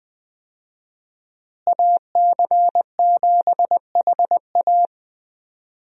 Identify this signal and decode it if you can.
Morse code, heard as a keyed tone, about 20 wpm, 700 Hz. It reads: AC7HA
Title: USA Call Signs - 5 Characters - Repeated 2x - 20wpm